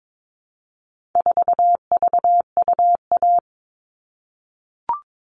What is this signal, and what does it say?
Morse code, heard as a keyed tone, about 22 wpm, 700 Hz. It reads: 4VUA
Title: All Cut Numbers - 4-Digit 22wpm